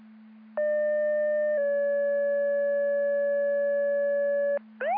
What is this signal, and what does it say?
Two-tone page, heard as a one-second tone, then a three-second tone, about 600 then 570 Hz. It is heard from 0.6 to 4.6 s.